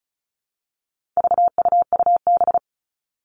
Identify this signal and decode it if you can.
Morse code, heard as a keyed tone, about 34 wpm, 700 Hz. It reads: VUUB
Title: All Cut Numbers - 4-Digit 35wpm